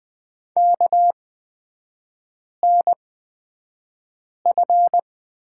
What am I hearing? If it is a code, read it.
Morse code, heard as a keyed tone, about 20 wpm, spaced as for 5 wpm, 700 Hz. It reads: KNF
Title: 243 Lesson 39e - Letter z - Three Characters 20wpm